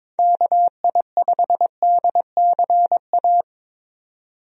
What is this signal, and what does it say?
Morse code, heard as a keyed tone, about 22 wpm, 700 Hz. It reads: KI5DCA
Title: USA Call Signs - 6 Characters - 22wpm